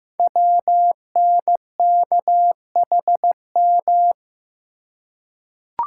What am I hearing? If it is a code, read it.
Morse code, heard as a keyed tone, about 15 wpm, 700 Hz. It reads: WNKHM
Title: Code Groups - 5 Letters - 15wpm